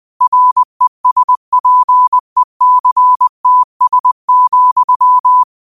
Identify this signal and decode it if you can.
Morse code, heard as a keyed tone, about 20 wpm, 1000 Hz. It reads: RESPECTS,